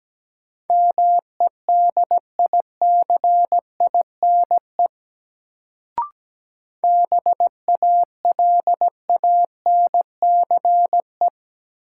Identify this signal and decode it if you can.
Morse code, heard as a keyed tone, about 17 wpm, 700 Hz. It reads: MEDICINE BALANCE
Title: Sentences from Top 900 Words - New Word Drill - 17wpm